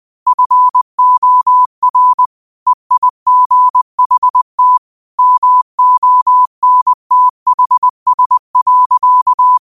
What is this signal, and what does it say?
Morse code, heard as a keyed tone, about 20 wpm, 1000 Hz. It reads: FOR EIGHT MONTHS.
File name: qotdfiles / 20wpm